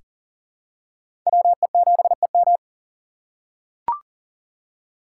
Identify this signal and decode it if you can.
Morse code, heard as a keyed tone, about 39 wpm, 700 Hz. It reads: WE7EM
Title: USA Call Signs - 5 Characters - 4x Character Spacing - 40wpm